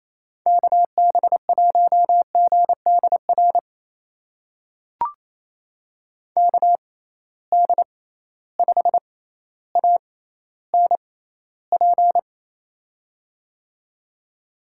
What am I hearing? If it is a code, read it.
Morse code, heard as a keyed tone, about 28 wpm, spaced as for 14 wpm, 700 Hz. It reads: KB1GDR KD5ANP
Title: USA Call Signs - 6 Characters - 6x Character Spacing - 28wpm